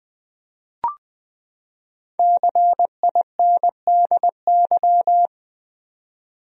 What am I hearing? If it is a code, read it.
Morse code, heard as a keyed tone, about 20 wpm, 700 Hz. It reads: CINDY
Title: USA Common Names - Max 5 Letters - 20wpm